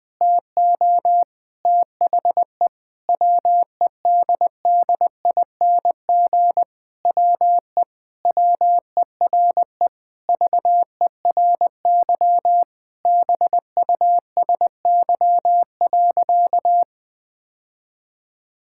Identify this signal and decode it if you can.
Morse code, heard as a keyed tone, about 20 wpm, 700 Hz. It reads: TO THE WEDDING WE WERE VERY BUSY.
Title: Sentences with 7-Word Phrases - Spoken Beginning - 20wpm